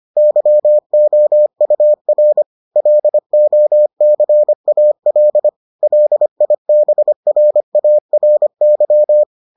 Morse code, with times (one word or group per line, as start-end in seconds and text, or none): 0.2-2.4 YOUR
2.8-5.5 LOCAL
5.8-9.2 LIBRARY